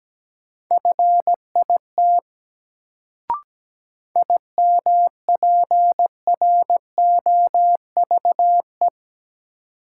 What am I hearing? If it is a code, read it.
Morse code, heard as a keyed tone, about 17 wpm, 700 Hz. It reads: FIT IMPROVE
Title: Sentences from Top 800 Words - New Word Drill - 17wpm